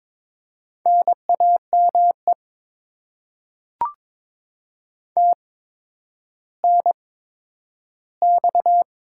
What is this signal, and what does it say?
Morse code, heard as a keyed tone, about 23 wpm, spaced as for 11 wpm, 700 Hz. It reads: NAME TNX
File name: Top 20 QSO Elements - 8x Character Spacing - 22wpm